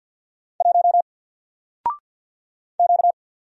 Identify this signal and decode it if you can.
Morse code, heard as a keyed tone, about 48 wpm, 700 Hz. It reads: 1 Q
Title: Single Letter-Number Alt 50wpm